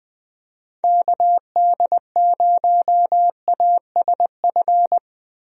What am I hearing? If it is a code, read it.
Morse code, heard as a keyed tone, about 20 wpm, 700 Hz. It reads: KD0ASF
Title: USA Call Signs - 6 Characters - Repeated 3x - 20wpm